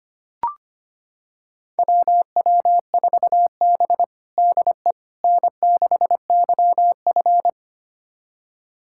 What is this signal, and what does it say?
Morse code, heard as a keyed tone, about 25 wpm, 700 Hz. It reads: WW4B DE N6YF
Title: USA Callsign DE Callsign - 4 Characters - 25wpm